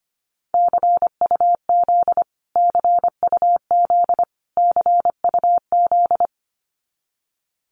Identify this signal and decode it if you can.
Morse code, heard as a keyed tone, about 25 wpm, 700 Hz. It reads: CUZ CUZ CUZ
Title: Abbreviations Repeated 3x - 25wpm